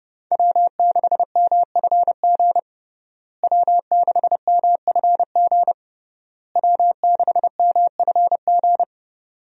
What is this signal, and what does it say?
Morse code, heard as a keyed tone, about 32 wpm, 700 Hz. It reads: W6MFG W6MFG W6MFG
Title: USA Call Signs - 5 Characters - Repeated 3x - 30wpm